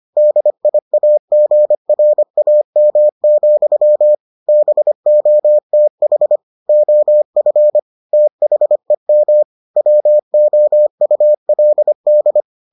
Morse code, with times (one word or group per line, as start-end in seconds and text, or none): 0.2-4.2 DIAGRAM,
4.5-6.4 BOTH
6.7-7.8 OF
8.1-9.4 THEM
9.8-12.4 WOULD